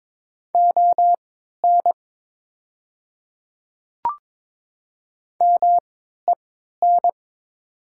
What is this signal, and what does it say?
Morse code, heard as a keyed tone, about 22 wpm, spaced as for 13 wpm, 700 Hz. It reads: ON MEN
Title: 3x Character Spacing - Top 200 Words - 22wpm